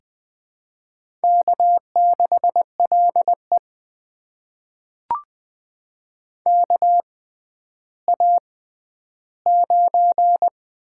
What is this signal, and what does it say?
Morse code, heard as a keyed tone, about 20 wpm, spaced as for 11 wpm, 700 Hz. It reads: K6LE KA9
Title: USA Call Signs - 4 Characters - 6x Character Spacing - 20wpm